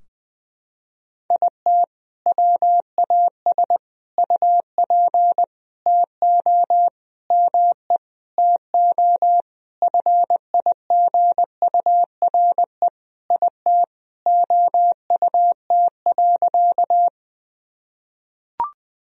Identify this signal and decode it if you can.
Morse code, heard as a keyed tone, about 20 wpm, 700 Hz. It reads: IT WAS UP TO ME TO FIGURE IT OUT.
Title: Sentences from Top 1000 Words - Review Part 2 - 20wpm